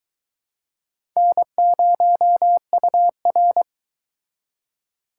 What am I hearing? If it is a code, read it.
Morse code, heard as a keyed tone, about 22 wpm, 700 Hz. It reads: N0UR